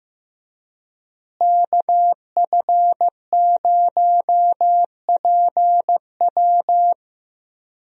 Morse code, heard as a keyed tone, about 16 wpm, 700 Hz. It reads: KF0PW
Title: USA Call Signs - 5 Characters - Repeated 3x - 15wpm